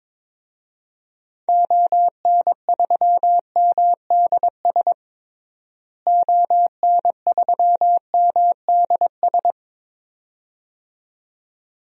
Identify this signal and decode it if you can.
Morse code, heard as a keyed tone, about 22 wpm, 700 Hz. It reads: ON3MDS ON3MDS
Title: DXCC-100 Call Signs - 6 Characters - Repeated 2x - 22wpm